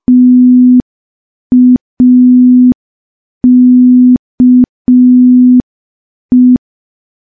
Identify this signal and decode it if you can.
Morse code, heard as a keyed tone, about 5 wpm, 250 Hz. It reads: TAKE
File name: morse / morse-training-05WPM